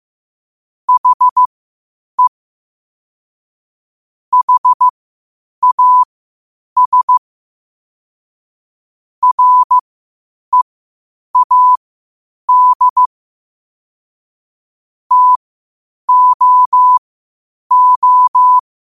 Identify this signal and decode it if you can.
Morse code, heard as a keyed tone, about 15 wpm, spaced as for 8 wpm, 1000 Hz. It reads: HE HAS READ TOO
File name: qotdfiles / 8wpm